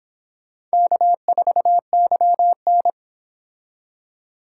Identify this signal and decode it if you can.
Morse code, heard as a keyed tone, about 26 wpm, 700 Hz. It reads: K4YN